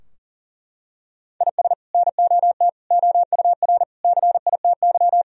Morse code, heard as a keyed tone, about 40 wpm, 700 Hz. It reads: IS NOT OUR CITY
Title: Sentences from Top 1000 Words - Review Part 3 - 40wpm